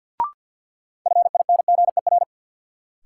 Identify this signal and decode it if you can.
Morse code, heard as a keyed tone, about 52 wpm, 700 Hz. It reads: FINGER
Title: Sentences from Top 1600 Words - New Word Drill Alt - 50wpm